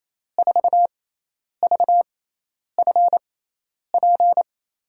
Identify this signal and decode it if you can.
Morse code, heard as a keyed tone, about 28 wpm, spaced as for 10 wpm, 700 Hz. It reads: 4VFP
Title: USA Call Signs - 6x Character Spacing - 28wpm